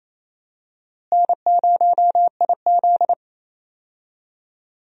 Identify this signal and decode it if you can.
Morse code, heard as a keyed tone, about 28 wpm, 700 Hz. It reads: N0IZ
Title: USA Call Signs - 4 Characters - Repeated 3x - 28wpm